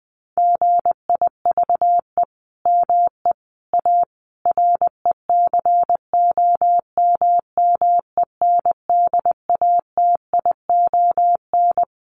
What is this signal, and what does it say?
Morse code, heard as a keyed tone, about 20 wpm, 700 Hz. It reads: GIVE ME A RECOMMENDATION
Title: Sentences from Top 1400 Words 20wpm